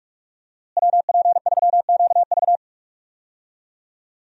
Morse code, heard as a keyed tone, about 42 wpm, 700 Hz. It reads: WP3QV
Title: USA Call Signs - 5 Characters - Repeated 3x - 45wpm